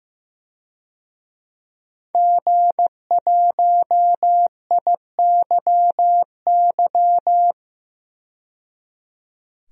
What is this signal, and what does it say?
Morse code, heard as a keyed tone, about 15 wpm, 700 Hz. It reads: G1IYY
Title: DXCC-100 Call Signs - 5 Characters - 15wpm